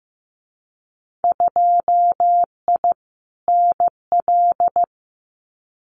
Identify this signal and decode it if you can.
Morse code, heard as a keyed tone, about 15 wpm, 700 Hz. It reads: 2I NL
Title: Winter Field Day Exchanges Encore 15wpm